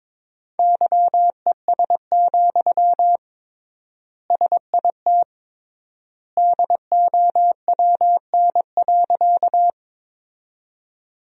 Morse code, heard as a keyed tone, about 22 wpm, 700 Hz. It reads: YES, SIT DOWN.
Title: 3x Word-Spacing - Sentences from Top 500 Words - 22wpm